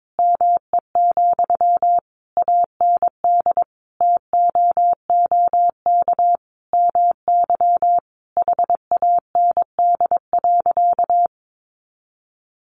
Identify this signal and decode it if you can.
Morse code, heard as a keyed tone, about 22 wpm, 700 Hz. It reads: ME, AND TOOK MY HAND.
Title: Sentences from Top 1100 Words 22wpm